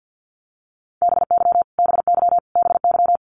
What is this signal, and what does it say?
Morse code, heard as a keyed tone, about 50 wpm, 700 Hz. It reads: BY BY BY
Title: Top 200 Words Repeated 50wpm